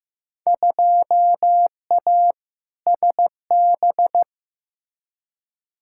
Morse code, heard as a keyed tone, about 15 wpm, 700 Hz. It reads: 2A SB
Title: Field Day Exchanges - 15wpm